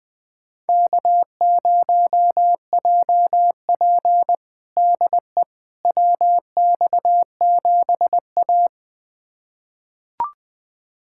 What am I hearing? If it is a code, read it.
Morse code, heard as a keyed tone, about 20 wpm, 700 Hz. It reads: K0JP DE WX7A
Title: USA Callsign DE Callsign - 4 Characters - 20wpm